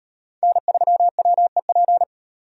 Morse code, heard as a keyed tone, about 34 wpm, 700 Hz. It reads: N3WEP